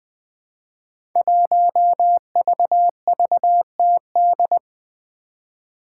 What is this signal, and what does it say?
Morse code, heard as a keyed tone, about 20 wpm, 700 Hz. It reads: 1VVTD
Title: Five Challenging Letter-Number Combinations 20wpm